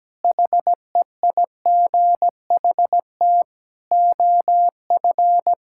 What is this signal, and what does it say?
Morse code, heard as a keyed tone, about 17 wpm, 700 Hz. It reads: HEIGHT OF
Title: Sentences from Top 1600 Words 17wpm